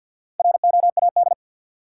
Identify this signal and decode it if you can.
Morse code, heard as a keyed tone, about 48 wpm, 700 Hz. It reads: ROAD
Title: Sentences from Top 400 Words - New Word Drill - 50wpm